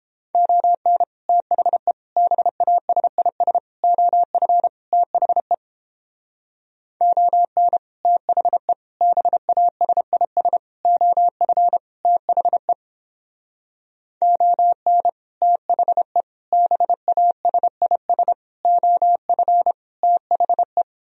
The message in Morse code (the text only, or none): ON THE BASIS OF THE ON THE BASIS OF THE ON THE BASIS OF THE